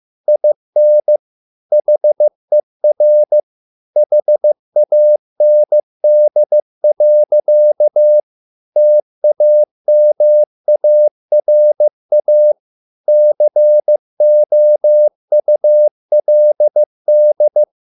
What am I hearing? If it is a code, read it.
Morse code, heard as a keyed tone, about 15 wpm, 600 Hz. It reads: IN HER HAND. TAMARA COULD